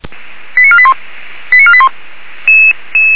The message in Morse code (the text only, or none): none